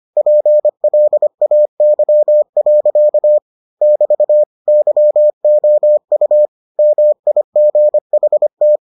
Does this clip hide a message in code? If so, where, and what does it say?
Yes: Morse code from 0.2 to 8.8 s, heard as a keyed tone, about 25 wpm, 600 Hz. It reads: PLAY. <BT> YOU MIGHT